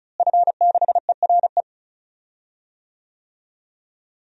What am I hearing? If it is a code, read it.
Morse code, heard as a keyed tone, about 36 wpm, 700 Hz. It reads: F6ERE
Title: DXCC-100 Call Signs - 5 Characters - Repeated 3x - 35wpm